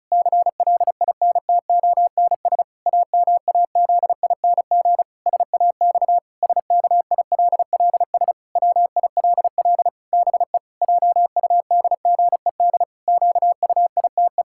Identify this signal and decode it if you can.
Morse code, heard as a keyed tone, about 36 wpm, 700 Hz. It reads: CLINTONS AMAZING SAX SKILLS WILL BE JUDGED QUITE
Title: Pangram Challenge v2 - 35wpm